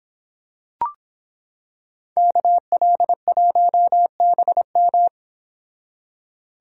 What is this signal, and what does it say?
Morse code, heard as a keyed tone, about 26 wpm, 700 Hz. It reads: KL1BM